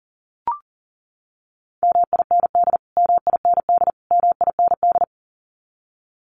Morse code, heard as a keyed tone, about 40 wpm, 700 Hz. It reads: MIND MIND MIND
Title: Top 500 Words Repeated 40wpm